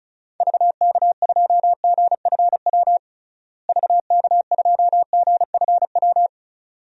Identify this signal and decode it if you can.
Morse code, heard as a keyed tone, about 36 wpm, 700 Hz. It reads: VK2GFW VK2GFW
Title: DXCC-100 Call Signs - 6 Characters - Repeated 2x - 35wpm